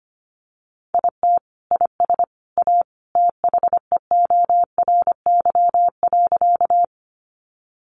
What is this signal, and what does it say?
Morse code, heard as a keyed tone, about 25 wpm, 700 Hz. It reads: IT IS A THEORY.
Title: Sentences from Top 600 Words 25wpm